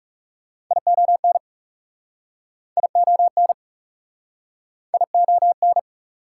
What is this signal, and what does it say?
Morse code, heard as a keyed tone, about 41 wpm, 690 Hz. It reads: ION ION ION